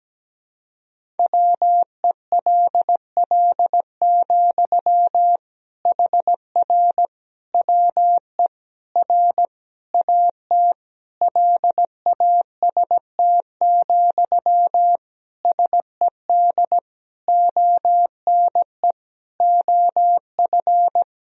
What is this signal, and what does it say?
Morse code, heard as a keyed tone, about 17 wpm, 700 Hz. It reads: WELL, HR WE R AT LAST, SED ONE OF
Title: Sentences from Top 300 Words and Abbreviations 17wpm